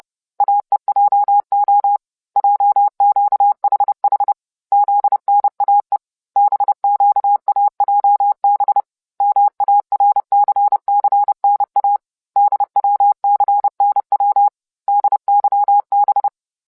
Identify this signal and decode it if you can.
Morse code, heard as a keyed tone, about 30 wpm, 810 Hz. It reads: AEJO JQHH ZNAE BQAJB MARCCNA DWCNW DYB